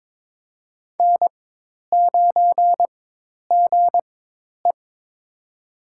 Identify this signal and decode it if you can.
Morse code, heard as a keyed tone, about 22 wpm, spaced as for 10 wpm, 700 Hz. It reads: N9GE
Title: USA Call Signs - 4x Character Spacing - 22wpm